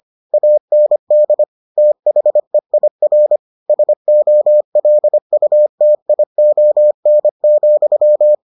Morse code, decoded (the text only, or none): AND THEIR SOLUTION,